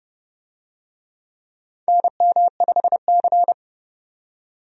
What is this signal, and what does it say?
Morse code, heard as a keyed tone, about 30 wpm, 700 Hz. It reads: NM5C
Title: USA Call Signs - 4 Characters - 30wpm